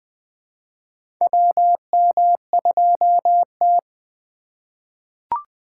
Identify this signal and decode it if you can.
Morse code, heard as a keyed tone, about 20 wpm, 700 Hz. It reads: WM2T
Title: USA Call Signs - 4 Characters - 4x Character Spacing - 20wpm